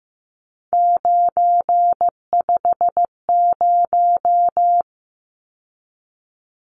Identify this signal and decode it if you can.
Morse code, heard as a keyed tone, about 15 wpm, 700 Hz. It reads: 950
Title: Numbers - 3-Digit 15wpm